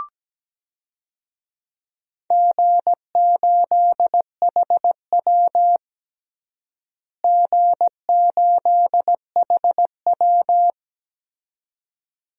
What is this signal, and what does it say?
Morse code, heard as a keyed tone, about 17 wpm, 700 Hz. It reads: G8HW G8HW
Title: DXCC-100 Call Signs - 4 Characters - Repeated 2x - 17wpm